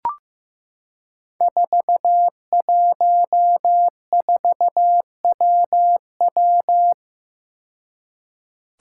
Morse code, heard as a keyed tone, about 15 wpm, 700 Hz. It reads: 414WW
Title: Five Challenging Letter-Number Combinations 15wpm